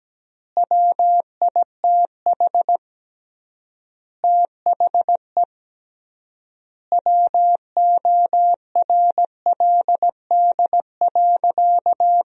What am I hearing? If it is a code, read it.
Morse code, heard as a keyed tone, about 17 wpm, 700 Hz. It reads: WITH THE WORLD.
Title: 3x Word-Spacing - Sentences from Top 200 Words - 17wpm